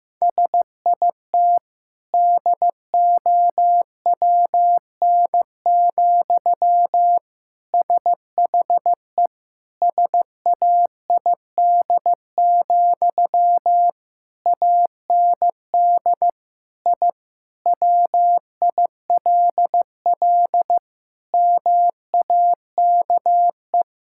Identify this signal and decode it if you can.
Morse code, heard as a keyed tone, about 15 wpm, 700 Hz. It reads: SIT DOWN, SHE SAID, AND I WILL MAKE
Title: Sentences from Top 1000 Words - Review Part 6 - 15wpm